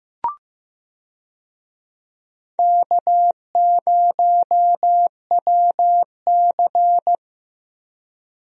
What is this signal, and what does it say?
Morse code, heard as a keyed tone, about 15 wpm, 700 Hz. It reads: K0WC
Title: USA Call Signs - 4 Characters - Repeated 3x - 15wpm